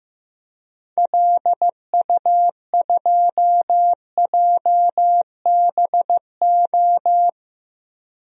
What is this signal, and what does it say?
Morse code, heard as a keyed tone, about 15 wpm, 700 Hz. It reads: LU2JBO
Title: DXCC-100 Call Signs - 6 Characters - 15wpm